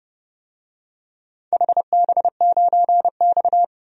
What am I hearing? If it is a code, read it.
Morse code, heard as a keyed tone, about 30 wpm, 700 Hz. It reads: HB9X
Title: DXCC-100 Call Signs - 4 Characters - 30wpm